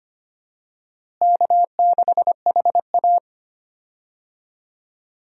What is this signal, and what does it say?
Morse code, heard as a keyed tone, about 25 wpm, 700 Hz. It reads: K6HA
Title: USA Call Signs - 4 Characters - Repeated 3x - 25wpm